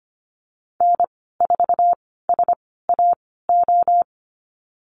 Morse code, heard as a keyed tone, about 25 wpm, spaced as for 16 wpm, 700 Hz. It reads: N4SAO
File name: Lesson 15g - Number 5 - Call Signs 25wpm